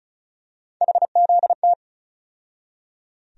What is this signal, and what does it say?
Morse code, heard as a keyed tone, about 36 wpm, 700 Hz. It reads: HZT
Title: USA Call Sign Postfixes - 3 Letters 35wpm